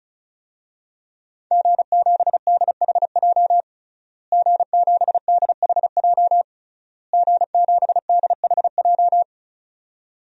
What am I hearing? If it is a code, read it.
Morse code, heard as a keyed tone, about 34 wpm, 700 Hz. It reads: G7DHJ G7DHJ G7DHJ
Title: DXCC-100 Call Signs - 5 Characters - Repeated 3x - 35wpm